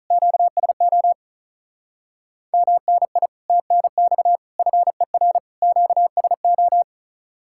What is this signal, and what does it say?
Morse code, heard as a keyed tone, about 44 wpm, 700 Hz. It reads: QSO MNI TNX FER QSO